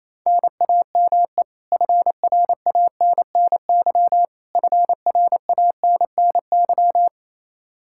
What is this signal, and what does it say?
Morse code, heard as a keyed tone, about 28 wpm, 700 Hz. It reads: NAME FRANNY FRANNY
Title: Practice QSO - Exchange 1 - Beginning - 28wpm